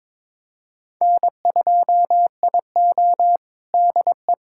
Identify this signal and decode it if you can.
Morse code, heard as a keyed tone, about 22 wpm, 700 Hz. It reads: N2IO DE
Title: USA Callsign DE Callsign - 4 Characters - 22wpm